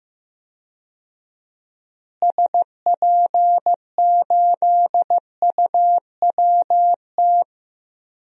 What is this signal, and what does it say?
Morse code, heard as a keyed tone, about 15 wpm, 700 Hz. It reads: SP8UWT
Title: DXCC-100 Call Signs - 6 Characters - Repeated 2x - 15wpm